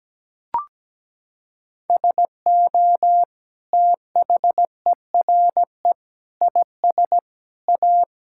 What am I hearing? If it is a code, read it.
Morse code, heard as a keyed tone, about 17 wpm, 700 Hz. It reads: SO THERE IS A
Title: Sentences from Top 1600 Words 17wpm